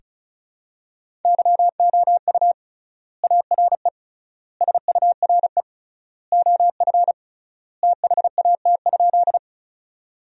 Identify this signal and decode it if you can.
Morse code, heard as a keyed tone, about 36 wpm, 700 Hz. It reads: YOU ARE SURE OF THAT?
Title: 3x Word-Spacing - Sentences from Top 300 Words - 35wpm